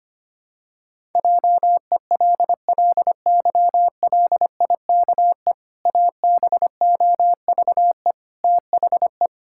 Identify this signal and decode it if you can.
Morse code, heard as a keyed tone, about 25 wpm, 700 Hz. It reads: JELLYLIKE ABOVE THE
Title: Pangram Challenge v2 - 25wpm